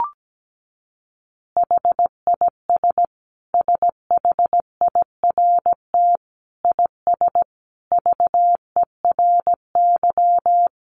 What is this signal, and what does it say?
Morse code, heard as a keyed tone, about 17 wpm, 700 Hz. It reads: HIS SHIRT IS VERY
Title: Additional Easy Phrases 17wpm